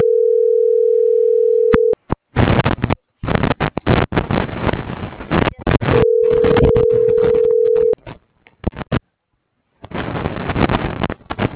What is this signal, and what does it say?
Ringback tone (North America), 440 + 480 Hz.